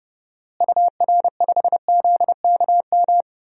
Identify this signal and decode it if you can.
Morse code, heard as a keyed tone, about 30 wpm, 700 Hz. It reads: UR5ZKM
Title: DXCC-100 Call Signs - 6 Characters - Repeated 3x - 30wpm